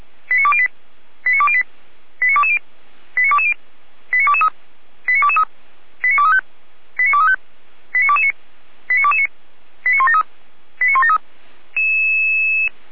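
Five-tone selective call (ZVEI), heard as quick runs of five tones, tones of about 70 ms. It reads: 89208 89208 89220 89220 89223 89223 89235 89235 89229 89229 89173 89173
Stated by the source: ZVEI five-tone selective call 89169 received by SDR.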